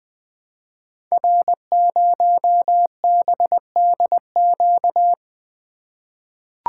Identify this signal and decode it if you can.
Morse code, heard as a keyed tone, about 20 wpm, 700 Hz. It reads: R0BDQ
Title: Code Groups - 5 Letter:Numbers - 20wpm